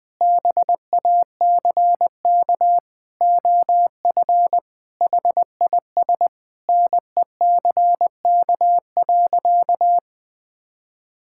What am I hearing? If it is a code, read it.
Morse code, heard as a keyed tone, about 20 wpm, 700 Hz. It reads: BACK OF HIS NECK.
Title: Sentences with 4-Word Phrases - 20wpm